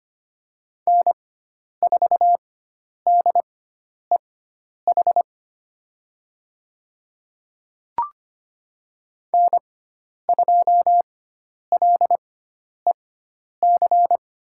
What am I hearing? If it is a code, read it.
Morse code, heard as a keyed tone, about 25 wpm, spaced as for 10 wpm, 700 Hz. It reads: N4DEH N2LEC
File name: Lesson 16g - Letter c - Call Signs 25wpm